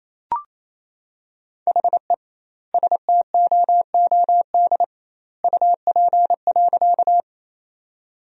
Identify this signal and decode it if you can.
Morse code, heard as a keyed tone, about 28 wpm, 700 Hz. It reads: HE STOOD UP.
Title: 2x Word-Spacing - Sentences from Top 500 Words - 28wpm